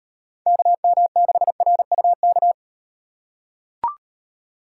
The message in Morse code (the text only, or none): KM6RUK